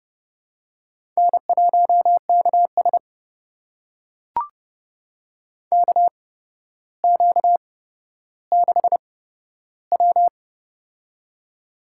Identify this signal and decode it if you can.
Morse code, heard as a keyed tone, about 30 wpm, spaced as for 13 wpm, 700 Hz. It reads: N1KS KQ6W
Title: USA Call Signs - 4 Characters - 8x Character Spacing - 30wpm